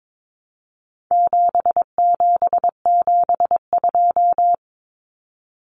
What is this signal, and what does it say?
Morse code, heard as a keyed tone, about 22 wpm, 700 Hz. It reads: 7772
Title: Numbers - 4-Digit 22wpm